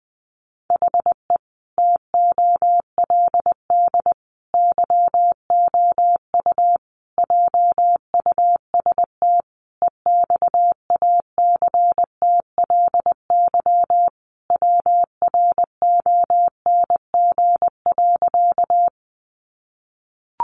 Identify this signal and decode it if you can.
Morse code, heard as a keyed tone, about 20 wpm, 700 Hz. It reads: HE TOLD YOU JUST EXACTLY WRONG.
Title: Sentences from Top 900 Words 20wpm